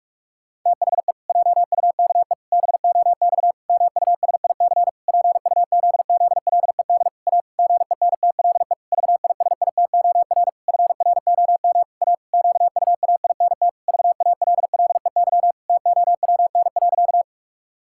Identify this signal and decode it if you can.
Morse code, heard as a keyed tone, about 41 wpm, 700 Hz. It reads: THE JUKE BOX MUSIC PUZZLED A GENTLE VISITOR FROM A QUAINT VALLEY TOWN.
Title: Pangram Challenge v2 - 45wpm